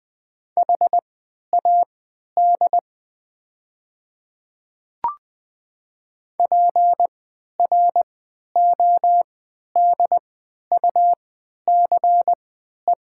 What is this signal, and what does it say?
Morse code, heard as a keyed tone, about 20 wpm, spaced as for 11 wpm, 700 Hz. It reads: HAD PRODUCE
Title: 3x Character Spacing - Top 500 Words - 20wpm